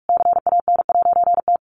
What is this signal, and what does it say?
Morse code, heard as a keyed tone, about 45 wpm, 700 Hz. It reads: CAN'T
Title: A Sound Of Thunder 45wpm